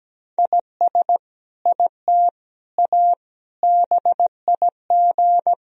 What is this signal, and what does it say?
Morse code, heard as a keyed tone, about 17 wpm, 700 Hz. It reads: IS IT A BIG
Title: Sentences from Top 1000 Words - Review Part 3 - 17wpm